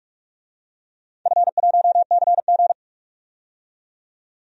Morse code, heard as a keyed tone, about 46 wpm, 700 Hz. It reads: F1CG
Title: DXCC-100 Call Signs - 4 Characters - 45wpm